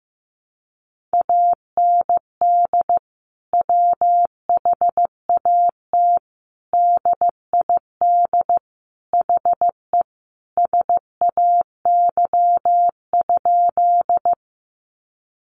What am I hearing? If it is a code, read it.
Morse code, heard as a keyed tone, about 15 wpm, 700 Hz. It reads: AND WHAT DID HE SAY?
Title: Sentences from Top 200 Words 15wpm